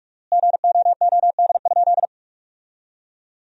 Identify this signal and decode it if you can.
Morse code, heard as a keyed tone, about 46 wpm, 700 Hz. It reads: GOOD?
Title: 4x Word-Spacing - Sentences from Top 200 Words - 45wpm